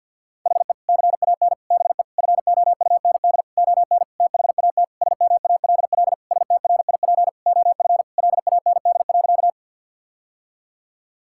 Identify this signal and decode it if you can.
Morse code, heard as a keyed tone, about 51 wpm, 700 Hz. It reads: HE CAN BE FOUND ON THAT SMALL STRIP OF LAND.
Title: Sentences from Top 1600 Words 50wpm